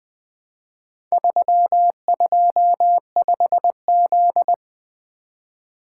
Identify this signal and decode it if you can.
Morse code, heard as a keyed tone, about 20 wpm, 700 Hz. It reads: 325Z
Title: Four Letter-Number Combinations - 20wpm